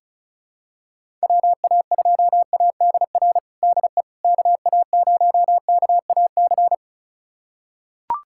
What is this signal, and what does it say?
Morse code, heard as a keyed tone, about 36 wpm, 690 Hz. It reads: WA2ADR DE KA0KAC
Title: USA Callsign DE Callsign - 6 Characters - 35wpm